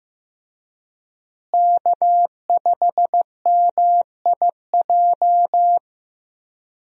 Morse code, heard as a keyed tone, about 15 wpm, 700 Hz. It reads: K5MIJ
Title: USA Call Signs - 5 Characters - Repeated 3x - 15wpm